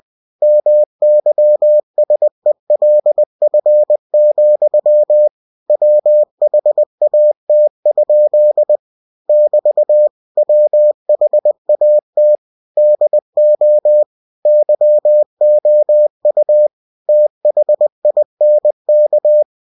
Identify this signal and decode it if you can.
Morse code, heard as a keyed tone, about 20 wpm, 600 Hz. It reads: MYSELF, WHAT? <BT> WHAT DO YOU THINK
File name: winnie-20wpm